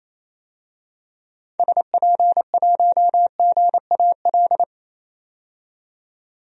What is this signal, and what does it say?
Morse code, heard as a keyed tone, about 28 wpm, 700 Hz. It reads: SP1GAL
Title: DXCC-100 Call Signs - 6 Characters - 28wpm